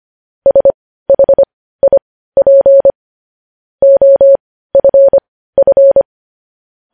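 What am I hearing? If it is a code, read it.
Morse code, heard as a keyed tone, about 25 wpm, spaced as for 15 wpm, 550 Hz. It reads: SHIP OFF
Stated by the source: Morse code practice of news headlines at 15 wpm.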